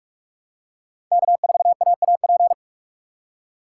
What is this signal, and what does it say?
Morse code, heard as a keyed tone, about 46 wpm, 700 Hz. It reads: K4AAP
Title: USA Call Signs - 5 Characters - 6x Character Spacing - 45wpm